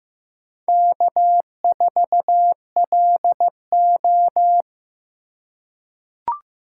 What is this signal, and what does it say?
Morse code, heard as a keyed tone, about 15 wpm, 700 Hz. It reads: K4LO
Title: USA Call Signs - 4 Characters - 4x Character Spacing - 15wpm